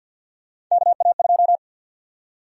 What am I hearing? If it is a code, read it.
Morse code, heard as a keyed tone, about 52 wpm, 700 Hz. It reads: KA2
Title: USA Call Sign Prefixes - 50wpm